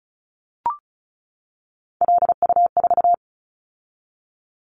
Morse code, heard as a keyed tone, about 36 wpm, 700 Hz. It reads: LU4
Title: Callsign Prefixes for Top 100 DXCC - 35wpm